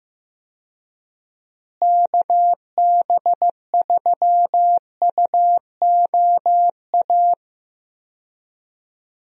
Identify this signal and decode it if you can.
Morse code, heard as a keyed tone, about 16 wpm, 700 Hz. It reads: KB3UOA
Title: USA Call Signs - 6 Characters - Repeated 2x - 15wpm